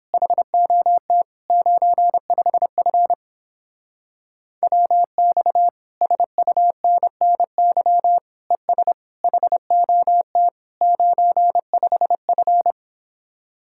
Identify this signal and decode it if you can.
Morse code, heard as a keyed tone, about 29 wpm, 700 Hz. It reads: HOT 95F WX SUNNY ES HOT 95F